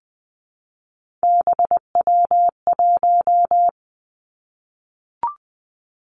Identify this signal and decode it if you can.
Morse code, heard as a keyed tone, about 20 wpm, 700 Hz. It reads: BW1
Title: Three Challenging Letter-Number Combinations 20wpm